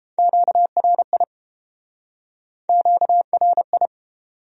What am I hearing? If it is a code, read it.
Morse code, heard as a keyed tone, about 30 wpm, 700 Hz. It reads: QRI QRI